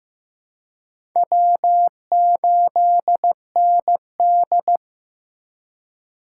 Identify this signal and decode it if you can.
Morse code, heard as a keyed tone, about 15 wpm, 700 Hz. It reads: W8ND
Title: USA Call Signs - 4 Characters - 2x Character Spacing - 15wpm